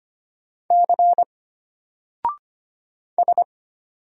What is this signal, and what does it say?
Morse code, heard as a keyed tone, about 25 wpm, 700 Hz. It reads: C S
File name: Single-Letter-Number-25wpm